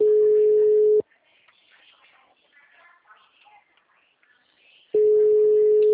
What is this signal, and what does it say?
Ringback tone (Europe, 425 Hz).